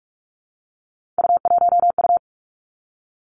Morse code, heard as a keyed tone, about 46 wpm, 700 Hz. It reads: U1U
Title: Three Challenging Letter-Number Combinations 45wpm